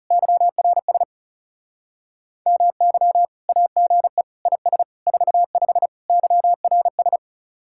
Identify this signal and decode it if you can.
Morse code, heard as a keyed tone, about 36 wpm, 700 Hz. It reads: YRS MY AGE IS 45 YRS
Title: Speed-Racing Common QSO Phrases Encore - 30wpm